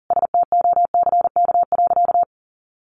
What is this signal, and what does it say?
Morse code, heard as a keyed tone, about 40 wpm, 700 Hz. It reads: STOCK.
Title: Sentences from Top 1300 Words 40wpm